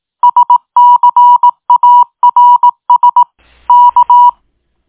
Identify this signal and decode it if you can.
Morse code, heard as a keyed tone, about 18 wpm, 970 Hz. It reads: SCARS K